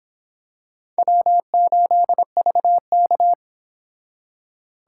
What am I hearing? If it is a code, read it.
Morse code, heard as a keyed tone, about 26 wpm, 700 Hz. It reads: W8VK